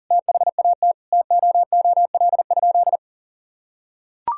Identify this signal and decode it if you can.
Morse code, heard as a keyed tone, about 40 wpm, 700 Hz. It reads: THAT TOOL?
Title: Sentences from Top 1000 Words - Review Part 6 - 40wpm